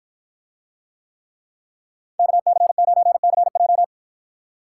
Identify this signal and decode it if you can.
Morse code, heard as a keyed tone, about 53 wpm, 700 Hz. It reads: KC9CJ